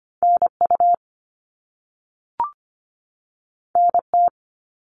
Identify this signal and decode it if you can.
Morse code, heard as a keyed tone, about 25 wpm, 700 Hz. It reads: NU NT
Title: Canadian Provinces and Territories 25wpm